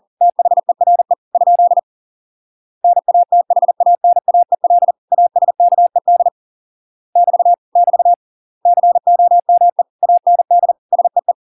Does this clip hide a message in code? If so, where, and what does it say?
Yes: Morse code from 0.2 to 11.3 s, heard as a keyed tone, about 40 wpm, 700 Hz. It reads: THERE ? NATHANAEL ASKED <BT> <BT> COME AND SEE